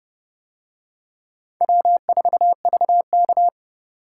Morse code, heard as a keyed tone, about 29 wpm, 700 Hz. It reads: W4VK